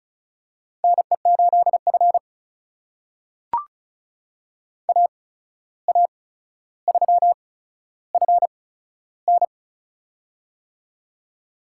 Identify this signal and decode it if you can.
Morse code, heard as a keyed tone, about 36 wpm, spaced as for 14 wpm, 700 Hz. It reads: NE8F AA3FN
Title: USA Call Signs - 8x Character Spacing - 35wpm